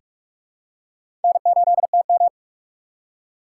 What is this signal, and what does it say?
Morse code, heard as a keyed tone, about 46 wpm, 700 Hz. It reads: N8TM
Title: USA Call Signs - 4 Characters - 4x Character Spacing - 45wpm